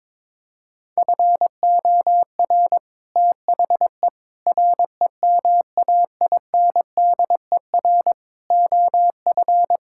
Morse code, heard as a keyed tone, about 22 wpm, 700 Hz. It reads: FOR THE REMAINDER OF
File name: practice / 22wpm / Sets of 6 Words - 22wpm